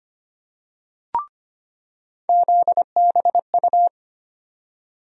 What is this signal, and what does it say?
Morse code, heard as a keyed tone, about 25 wpm, 700 Hz. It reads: ZBU
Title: USA Call Sign Postfixes - 3 Letters 25wpm